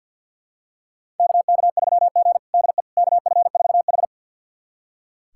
Morse code, heard as a keyed tone, about 45 wpm, 700 Hz. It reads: KK3G DE KF4H
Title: USA Callsign DE Callsign - 4 Characters - 50wpm